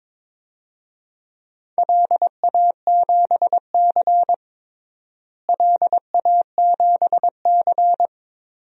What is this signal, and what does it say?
Morse code, heard as a keyed tone, about 22 wpm, 700 Hz. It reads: LA7C LA7C
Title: DXCC-100 Call Signs - 4 Characters - Repeated 2x - 22wpm